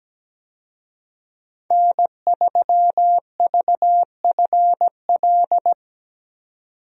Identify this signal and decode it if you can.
Morse code, heard as a keyed tone, about 17 wpm, 700 Hz. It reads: N3VFL
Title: USA Call Signs - 5 Characters - 17wpm